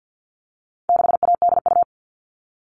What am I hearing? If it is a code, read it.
Morse code, heard as a keyed tone, about 50 wpm, 700 Hz. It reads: 6ADU
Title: All Cut Numbers - 4-Digit 50wpm